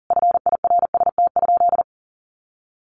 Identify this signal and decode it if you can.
Morse code, heard as a keyed tone, about 40 wpm, 700 Hz. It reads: FIRST?
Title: Sentences from Top 100 Words 40wpm